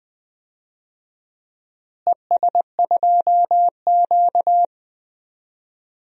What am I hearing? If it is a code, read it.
Morse code, heard as a keyed tone, about 20 wpm, 700 Hz. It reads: ES2Q
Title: DXCC-100 Call Signs - 4 Characters - Repeated 2x - 20wpm